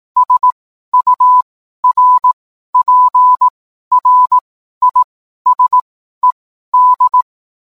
Morse code, heard as a keyed tone, about 18 wpm, spaced as for 12 wpm, 1000 Hz. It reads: SURPRISED